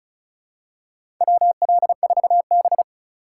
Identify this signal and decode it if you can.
Morse code, heard as a keyed tone, about 36 wpm, 700 Hz. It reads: WL4B
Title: USA Call Signs - 4 Characters - 4x Character Spacing - 35wpm